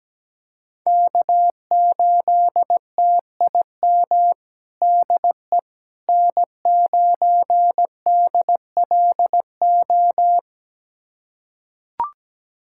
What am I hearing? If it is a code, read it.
Morse code, heard as a keyed tone, about 17 wpm, 700 Hz. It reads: K8TIM DE N9DLO
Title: USA Callsign DE Callsign - 5 Characters - 17wpm